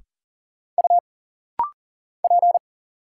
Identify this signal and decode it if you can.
Morse code, heard as a keyed tone, about 40 wpm, 700 Hz. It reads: U P
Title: Single Letter-Number Alt - Rapid-Fire 40wpm